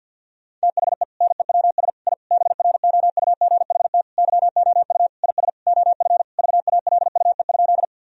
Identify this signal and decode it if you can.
Morse code, heard as a keyed tone, about 50 wpm, 700 Hz. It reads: THE NEWS I BROUGHT YOU IS OF VALUE?
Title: Sentences from Top 1000 Words - Review Part 3 - 50wpm